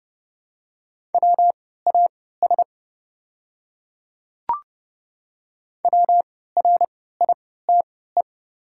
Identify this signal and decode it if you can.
Morse code, heard as a keyed tone, about 30 wpm, spaced as for 17 wpm, 700 Hz. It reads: WAS WRITE
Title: 3x Character Spacing - Top 100 Words - 30wpm